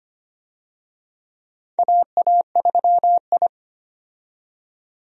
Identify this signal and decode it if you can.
Morse code, heard as a keyed tone, about 25 wpm, 700 Hz. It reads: AA3I
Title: USA Call Signs - 4 Characters - 25wpm